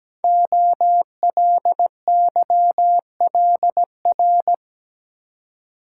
Morse code, heard as a keyed tone, about 17 wpm, 700 Hz. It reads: OLYLR
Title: Code Groups - 5 Letter:Numbers - 17wpm